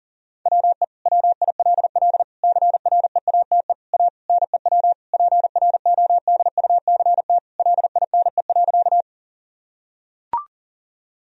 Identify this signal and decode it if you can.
Morse code, heard as a keyed tone, about 37 wpm, 700 Hz. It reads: WE WILL CREATE A NEW PRODUCT LINE.
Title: Sentences from Top 1000 Words - Review Part 3 - 40wpm